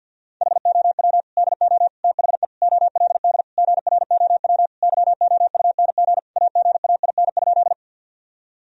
Morse code, heard as a keyed tone, about 50 wpm, 700 Hz. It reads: HOW DO THE OLD GROW YOUNG AGAIN?
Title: Sentences from Top 1000 Words - Review Part 1 - 50wpm